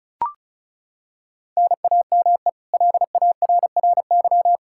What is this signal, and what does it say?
Morse code, heard as a keyed tone, about 34 wpm, 700 Hz. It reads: NAME LARRY
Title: Practice QSO - Exchange 1 - Beginning - 35wpm